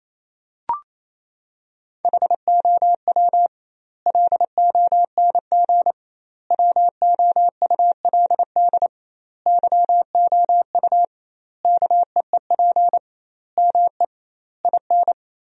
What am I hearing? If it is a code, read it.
Morse code, heard as a keyed tone, about 28 wpm, 700 Hz. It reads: HOW LONG WOULD YOU KEEP ME IN
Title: 2x Word-Spacing - Sentences from Top 500 Words - 28wpm